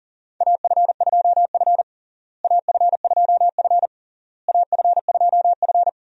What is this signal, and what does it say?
Morse code, heard as a keyed tone, about 40 wpm, 700 Hz. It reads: AF2F AF2F AF2F
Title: USA Call Signs - 4 Characters - Repeated 3x - 40wpm